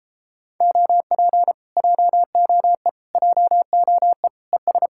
Morse code, heard as a keyed tone, about 33 wpm, 700 Hz. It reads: OP JOE JOE ES